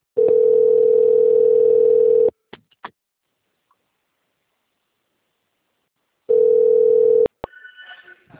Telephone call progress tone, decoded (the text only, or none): ringback tone